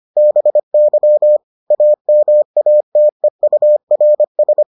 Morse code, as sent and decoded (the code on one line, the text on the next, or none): -... -.-- / .- -- .- - . ..- .-. ...
BY AMATEURS